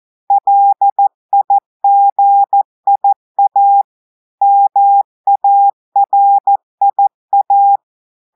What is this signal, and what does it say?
Morse code, heard as a keyed tone, about 14 wpm, 800 Hz. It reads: LIGIA MARIA
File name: audios / portugais / portugais texte en clair 1 vitesse 14 mots minutes